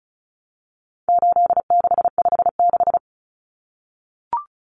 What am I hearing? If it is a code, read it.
Morse code, heard as a keyed tone, about 34 wpm, 700 Hz. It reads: 8656
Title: Common Cut Numbers - 4-Digit 35wpm